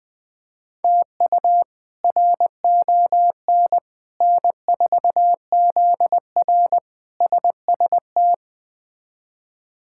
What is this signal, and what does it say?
Morse code, heard as a keyed tone, about 21 wpm, 700 Hz. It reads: TU RON N4ZR SST
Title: K1USN SST QSO Practice - 20wpm